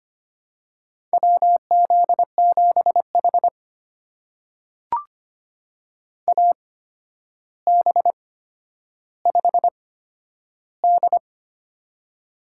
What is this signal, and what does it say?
Morse code, heard as a keyed tone, about 25 wpm, spaced as for 11 wpm, 700 Hz. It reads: WZ7H AB5D
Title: USA Call Signs - 4 Characters - 8x Character Spacing - 25wpm